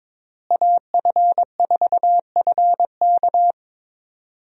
Morse code, heard as a keyed tone, about 22 wpm, 700 Hz. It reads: AF4FK
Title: USA Call Signs - 5 Characters - Repeated 3x - 22wpm